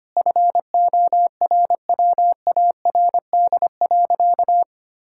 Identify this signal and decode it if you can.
Morse code, heard as a keyed tone, about 25 wpm, 700 Hz. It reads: FORWARD.
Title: Pangram Challenge v2 - 25wpm